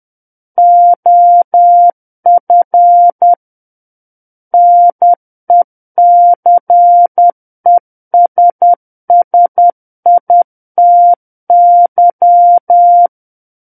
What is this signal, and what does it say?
Morse code, heard as a keyed tone, about 10 wpm, 700 Hz. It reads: OF NECESSITY
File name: Book One / 01-Chapter 1 - 10 WPM